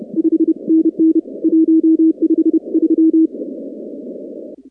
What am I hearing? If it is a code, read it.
Morse code, heard as a keyed tone, about 31 wpm, 320 Hz. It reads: 5NN 153
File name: contests / 2019-cw-wpx-kt5j-k5pi- -2330 0-19